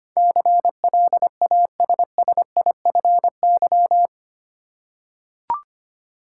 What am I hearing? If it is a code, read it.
Morse code, heard as a keyed tone, about 25 wpm, 700 Hz. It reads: CLASSIFY
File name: Eight Letter Words - 25wpm